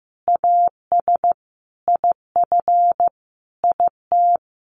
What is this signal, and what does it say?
Morse code, heard as a keyed tone, about 15 wpm, 700 Hz. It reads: AS IF IT
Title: Sentences from Top 1200 Words 15wpm